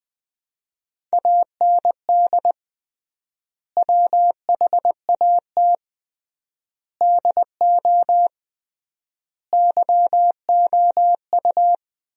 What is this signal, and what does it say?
Morse code, heard as a keyed tone, about 20 wpm, 700 Hz. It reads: AND WHAT DO YOU
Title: 3x Word-Spacing - Sentences from Top 100 Words - 20wpm